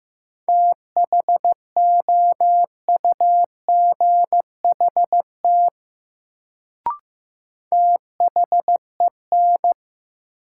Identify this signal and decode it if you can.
Morse code, heard as a keyed tone, about 15 wpm, 700 Hz. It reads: THOUGHT THEN
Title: Sets of 2 Words - New Word Drill - 15wpm